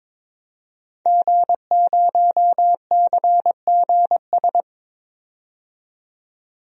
Morse code, heard as a keyed tone, about 22 wpm, 700 Hz. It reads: G0CGS
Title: DXCC-100 Call Signs - 5 Characters - 22wpm